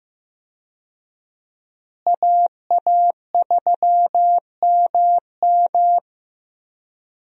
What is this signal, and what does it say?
Morse code, heard as a keyed tone, about 15 wpm, 700 Hz. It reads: AA3MM
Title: USA Call Signs - 5 Characters - Repeated 3x - 15wpm